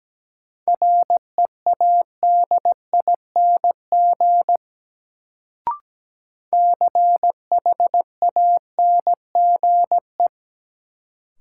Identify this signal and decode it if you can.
Morse code, heard as a keyed tone, about 17 wpm, 700 Hz. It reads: READING CHANGE
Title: Sentences from Top 600 Words - New Word Drill - 17wpm